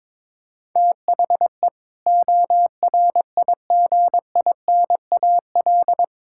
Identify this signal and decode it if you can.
Morse code, heard as a keyed tone, about 21 wpm, 700 Hz. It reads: THE ORIGINAL
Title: Sentences from Top 1000 Words - Review Part 6 - 22wpm